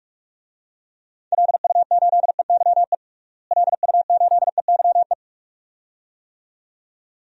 Morse code, heard as a keyed tone, about 46 wpm, 700 Hz. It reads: LU8EYE LU8EYE
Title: DXCC-100 Call Signs - 6 Characters - Repeated 2x - 45wpm